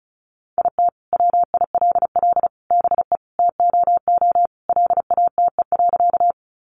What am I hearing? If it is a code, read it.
Morse code, heard as a keyed tone, about 34 wpm, 700 Hz. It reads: IT WILL BE TOO LATE.
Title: Sentences from Top 300 Words 35wpm